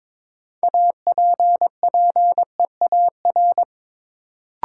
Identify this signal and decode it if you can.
Morse code, heard as a keyed tone, about 22 wpm, 700 Hz. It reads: APPEAR
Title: Sentences from Top 1000 Words - Review Part 2 - New Word Drill - 22wpm